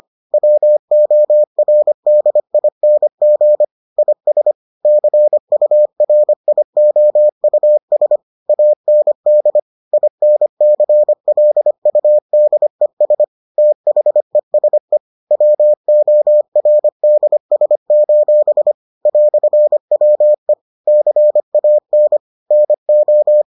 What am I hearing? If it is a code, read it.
Morse code, heard as a keyed tone, about 25 wpm, 600 Hz. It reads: WORDING IS CURIOUS AND INCLUDES THESE WORDS: "WE CAN NO